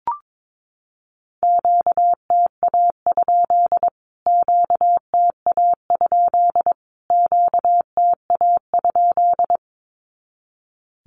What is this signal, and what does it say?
Morse code, heard as a keyed tone, about 22 wpm, 700 Hz. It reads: QTA? QTA? QTA?
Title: Q-Codes Repeated 3x 22wpm